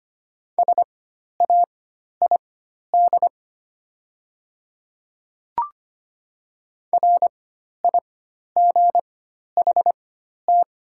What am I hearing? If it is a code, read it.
Morse code, heard as a keyed tone, about 25 wpm, spaced as for 12 wpm, 700 Hz. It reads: SAID RIGHT
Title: 4x Character Spacing - Top 200 Words - 25wpm